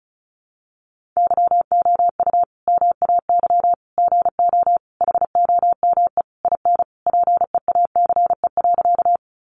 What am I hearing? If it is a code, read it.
Morse code, heard as a keyed tone, about 34 wpm, 700 Hz. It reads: YOU MAY GO HOME IN PEACE.
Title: Sentences from Top 1500 Words 35wpm